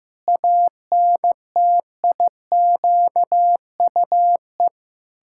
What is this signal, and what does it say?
Morse code, heard as a keyed tone, about 16 wpm, 700 Hz. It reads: ANTIQUE
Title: Pangram Challenge v2 - 15wpm